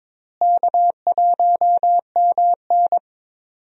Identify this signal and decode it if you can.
Morse code, heard as a keyed tone, about 22 wpm, 700 Hz. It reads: K1MN
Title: USA Call Signs - 4 Characters - Repeated 3x - 22wpm